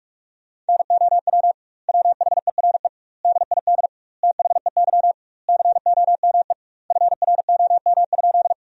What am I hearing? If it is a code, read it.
Morse code, heard as a keyed tone, about 46 wpm, 700 Hz. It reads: NOW WHERE DID THEY COME FROM?
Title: 2x Word-Spacing - Sentences from Top 200 Words - 45wpm